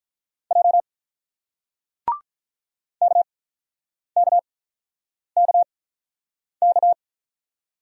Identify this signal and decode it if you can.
Morse code, heard as a keyed tone, about 48 wpm, spaced as for 9 wpm, 700 Hz. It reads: J KKKK